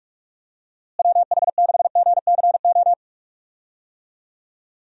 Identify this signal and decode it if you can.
Morse code, heard as a keyed tone, about 46 wpm, 700 Hz. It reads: WH6GCO
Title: USA Call Signs - 6 Characters - Repeated 3x - 45wpm